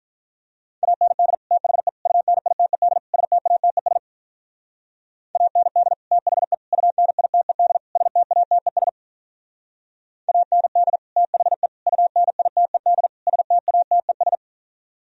Morse code, heard as a keyed tone, about 44 wpm, 700 Hz. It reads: AND THE UNITED STATES AND THE UNITED STATES AND THE UNITED STATES
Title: Speed-Racing - Sets of 4 Words 35wpm